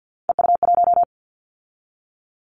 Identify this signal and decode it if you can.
Morse code, heard as a keyed tone, about 50 wpm, 700 Hz. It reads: EU1
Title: Callsign Prefixes for Top 50 DXCC - 50wpm